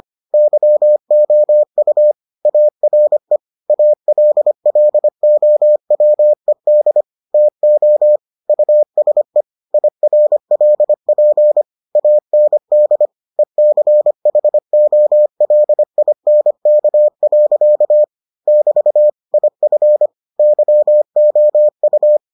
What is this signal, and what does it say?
Morse code, heard as a keyed tone, about 25 wpm, 600 Hz. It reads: YOU ARE ALLOWED TO USE IRLP AND ECHOLINK. <BT> IF YOU